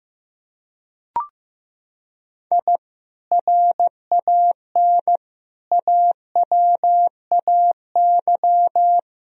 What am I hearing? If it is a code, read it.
Morse code, heard as a keyed tone, about 15 wpm, 700 Hz. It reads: I RAN AWAY
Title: Sentences from Top 1000 Words - Review Part 3 - 15wpm